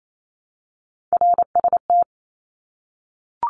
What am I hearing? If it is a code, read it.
Morse code, heard as a keyed tone, about 28 wpm, 700 Hz. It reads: RST
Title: Abbreviations Repeated 3x - 28wpm